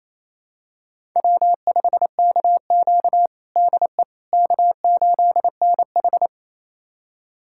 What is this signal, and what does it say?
Morse code, heard as a keyed tone, about 29 wpm, 690 Hz. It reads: W5KQ DE K8NH
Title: USA Callsign DE Callsign - 4 Characters - 28wpm